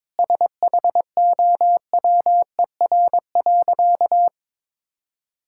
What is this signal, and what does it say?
Morse code, heard as a keyed tone, about 22 wpm, 700 Hz. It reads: SHOWER.
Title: Sentences from Top 1600 Words 22wpm